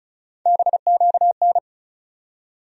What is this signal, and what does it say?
Morse code, heard as a keyed tone, about 34 wpm, 700 Hz. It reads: BQN
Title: USA Call Sign Postfixes - 3 Letters 35wpm